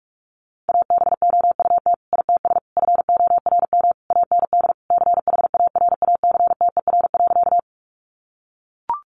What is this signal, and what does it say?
Morse code, heard as a keyed tone, about 44 wpm, 700 Hz. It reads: ABOUT ITS FORM AND CHARACTER.
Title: Sentences from Top 1200 Words 45wpm